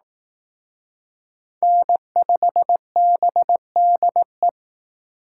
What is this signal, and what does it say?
Morse code, heard as a keyed tone, about 18 wpm, 700 Hz. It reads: N5BDE
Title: Speed Racing - USA 1x3 Call Signs 15wpm